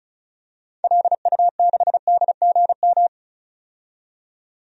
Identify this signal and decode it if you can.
Morse code, heard as a keyed tone, about 34 wpm, 700 Hz. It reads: LU6DGM
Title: DXCC-100 Call Signs - 6 Characters - 35wpm